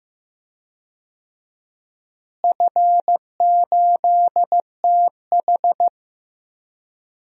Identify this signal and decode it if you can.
Morse code, heard as a keyed tone, about 15 wpm, 700 Hz. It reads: F8TH
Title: DXCC-100 Call Signs - 4 Characters - Repeated 2x - 15wpm